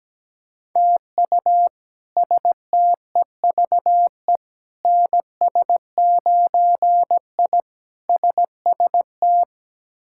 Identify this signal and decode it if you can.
Morse code, heard as a keyed tone, about 17 wpm, 700 Hz. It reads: TU STEVE NS9I SST
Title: K1USN SST QSO Practice - 17wpm